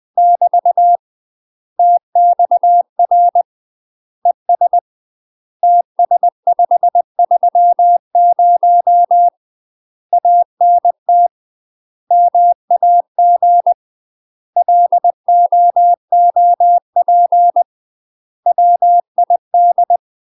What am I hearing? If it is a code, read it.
Morse code, heard as a keyed tone, about 20 wpm, 700 Hz. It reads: <BT> TXR ES TS530 ANT MAG LOOP WID
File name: UK MASTER QSOs TRAINING / 20WPM / UK MASTER QSOs 0023